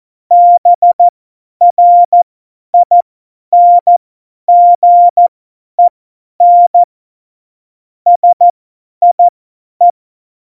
Morse code, heard as a keyed tone, about 14 wpm, spaced as for 10 wpm, 700 Hz. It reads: BRINGEN SIE
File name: Morsekurs / Witze / Witz 23. Juli 2019, 50 F70